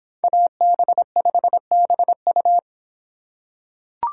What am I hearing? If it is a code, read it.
Morse code, heard as a keyed tone, about 26 wpm, 700 Hz. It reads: AB5BU